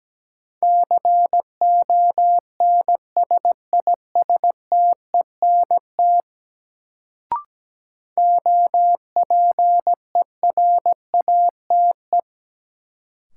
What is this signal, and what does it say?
Morse code, heard as a keyed tone, about 17 wpm, 700 Hz. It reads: CONSISTENT OPERATE
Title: Sentences from Top 1500 Words - New Word Drill Alt - 17wpm